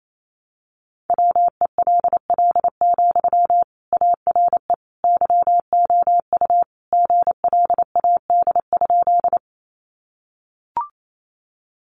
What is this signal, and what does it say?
Morse code, heard as a keyed tone, about 28 wpm, 700 Hz. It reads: WELL, ARE YOU GLAD?
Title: Sentences from Top 800 Words 28wpm